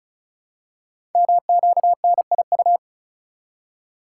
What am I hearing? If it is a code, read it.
Morse code, heard as a keyed tone, about 36 wpm, 690 Hz. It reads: MQNIU
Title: Code Groups - 5 Letters - 35wpm